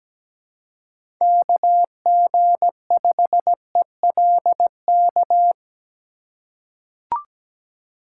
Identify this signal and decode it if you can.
Morse code, heard as a keyed tone, about 17 wpm, 700 Hz. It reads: KG5ELK
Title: USA Call Signs - 3x Character Spacing - 17wpm